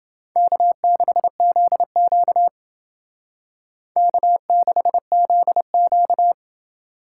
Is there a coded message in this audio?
Yes: Morse code from 0.4 to 6.3 s, heard as a keyed tone, about 29 wpm, 700 Hz. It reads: K6ZQ K6ZQ